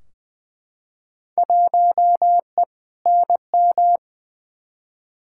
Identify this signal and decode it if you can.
Morse code, heard as a keyed tone, about 20 wpm, 700 Hz. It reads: 1E NM
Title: Field Day Exchanges - 20wpm